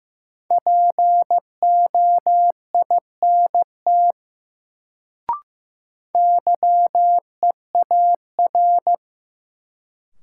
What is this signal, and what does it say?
Morse code, heard as a keyed tone, about 15 wpm, 700 Hz. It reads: POINT YEAR
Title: Sentences from Top 200 Words - New Word Drill - 15wpm